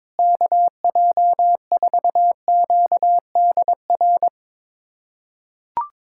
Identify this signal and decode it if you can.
Morse code, heard as a keyed tone, about 22 wpm, 700 Hz. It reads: KJ4QDR
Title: USA Callsign DE Callsign - 6 Characters - 22wpm